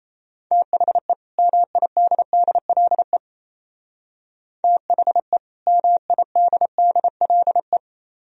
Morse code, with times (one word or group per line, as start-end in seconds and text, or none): 0.5-1.1 THE
1.4-3.2 MIDDLE
4.6-5.4 THE
5.7-7.8 MIDDLE